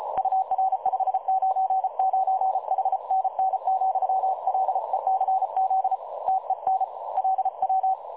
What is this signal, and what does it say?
Morse code, heard as a keyed tone, about 34 wpm, 760 Hz. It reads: UR 599 HNN,LECZ TEN LW
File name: dlazaawansowanych / nagraniazpasma / websdr recording 3565.0kHz